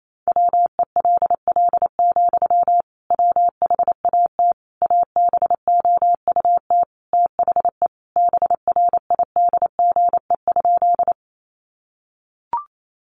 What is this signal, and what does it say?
Morse code, heard as a keyed tone, about 28 wpm, 700 Hz. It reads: WELL, WHAT ABOUT THE BRIDGE?
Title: Sentences from Top 1200 Words 28wpm